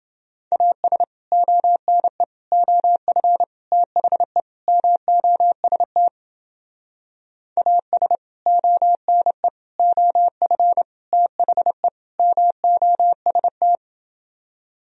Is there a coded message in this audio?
Yes: Morse code from 0.5 to 13.8 s, heard as a keyed tone, about 29 wpm, 700 Hz. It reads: AS ONE OF THE MOST AS ONE OF THE MOST